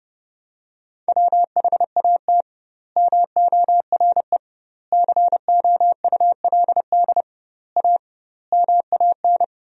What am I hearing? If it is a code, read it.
Morse code, heard as a keyed tone, about 32 wpm, 700 Hz. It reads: WHAT MORE COULD A MAN
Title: 2x Word-Spacing - Sentences from Top 200 Words - 30wpm